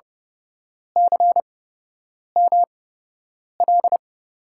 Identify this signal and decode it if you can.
Morse code, heard as a keyed tone, about 30 wpm, spaced as for 8 wpm, 700 Hz. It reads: CML
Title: USA Call Signs - 5 Characters - 8x Character Spacing - 30wpm